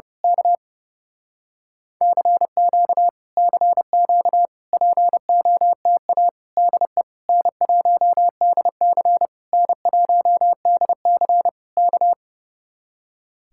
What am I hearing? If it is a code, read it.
Morse code, heard as a keyed tone, about 28 wpm, 700 Hz. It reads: K CQ CQ POTA DE N1DC N1DC K
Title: Speed-Racing Common QSO Phrases Encore - 30wpm